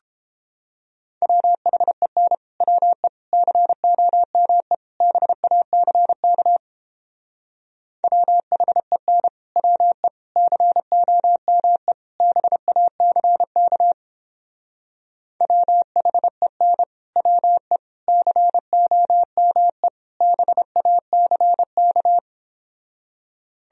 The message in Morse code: WHEN WE COME BACK WHEN WE COME BACK WHEN WE COME BACK

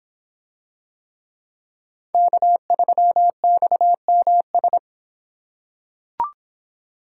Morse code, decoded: K3XMS

